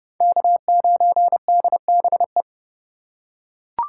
Morse code, heard as a keyed tone, about 31 wpm, 700 Hz. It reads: K9DBE